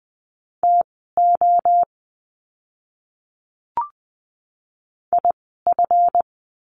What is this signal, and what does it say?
Morse code, heard as a keyed tone, about 20 wpm, spaced as for 14 wpm, 700 Hz. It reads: TO IF
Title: 237 Lesson 38f - Letter q - Words 20wpm